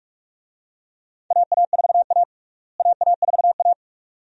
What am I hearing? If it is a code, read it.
Morse code, heard as a keyed tone, about 46 wpm, 700 Hz. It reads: AA4A AA4A
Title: USA Call Signs - 4 Characters - Repeated 2x - 45wpm